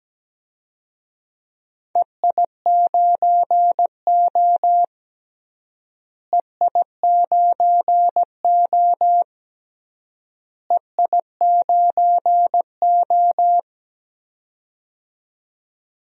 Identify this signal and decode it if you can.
Morse code, heard as a keyed tone, about 17 wpm, 700 Hz. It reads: EI9O EI9O EI9O
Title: DXCC-100 Call Signs - 4 Characters - Repeated 3x - 17wpm